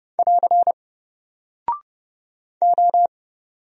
Morse code, heard as a keyed tone, about 30 wpm, 700 Hz. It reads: <AR> O
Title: Single Character (Letter:Number:Punctuation:Prosign) Alt 30wpm